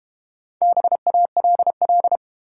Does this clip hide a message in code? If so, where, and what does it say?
Yes: Morse code from 0.6 to 2.2 s, heard as a keyed tone, about 32 wpm, 700 Hz. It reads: BALL